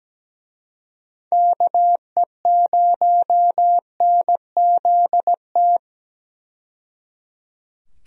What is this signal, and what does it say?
Morse code, heard as a keyed tone, about 17 wpm, 700 Hz. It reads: KE0NZT
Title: USA Call Signs - 6 Characters - 17wpm